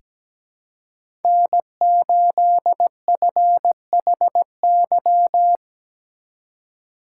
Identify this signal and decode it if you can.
Morse code, heard as a keyed tone, about 17 wpm, 700 Hz. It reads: N8FHY
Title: USA Call Signs - 5 Characters - 4x Character Spacing - 17wpm